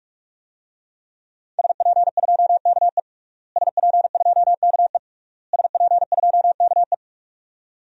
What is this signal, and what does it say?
Morse code, heard as a keyed tone, about 46 wpm, 700 Hz. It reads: SP2KE SP2KE SP2KE
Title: DXCC-100 Call Signs - 5 Characters - Repeated 3x - 45wpm